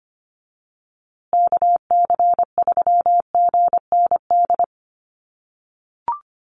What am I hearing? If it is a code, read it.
Morse code, heard as a keyed tone, about 25 wpm, 700 Hz. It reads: KC3GND
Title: USA Call Signs Repeated 2x 25wpm